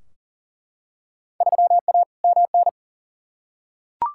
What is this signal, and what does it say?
Morse code, heard as a keyed tone, about 40 wpm, 700 Hz. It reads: 3A MN
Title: Field Day Exchanges - 40wpm